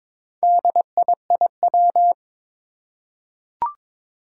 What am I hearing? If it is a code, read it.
Morse code, heard as a keyed tone, about 23 wpm, 700 Hz. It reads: DIIW
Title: Four Challenging Letter-Number Combinations 22wpm